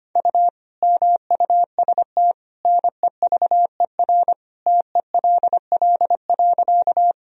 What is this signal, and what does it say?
Morse code, heard as a keyed tone, about 25 wpm, 700 Hz. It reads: U MUST NEVER TELL.
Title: Sentences from Top 300 Words and Abbreviations 25wpm